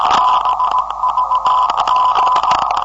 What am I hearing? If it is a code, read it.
Morse code, heard as a keyed tone, about 17 wpm, 1010 Hz, with noise. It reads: BVG
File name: 472khz web swl / 05october2012 / ok2bvg 472khz